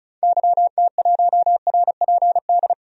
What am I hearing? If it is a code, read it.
Morse code, heard as a keyed tone, about 35 wpm, 700 Hz. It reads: YT1RPD
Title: DXCC-100 Call Signs - 6 Characters - 35wpm